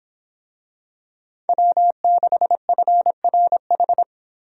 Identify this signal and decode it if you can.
Morse code, heard as a keyed tone, about 26 wpm, 700 Hz. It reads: W6FRH